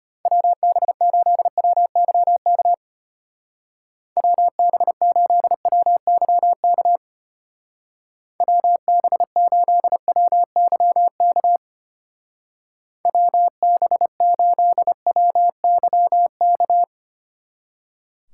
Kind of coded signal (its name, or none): Morse code